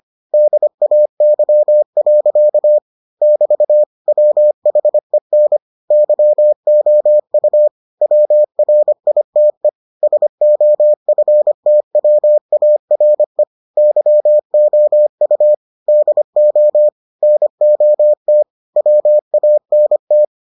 Morse code, heard as a keyed tone, about 25 wpm, 600 Hz. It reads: DAY. <BT> WHEN YOU WRITE SOFTWARE YOU DO NOT WANT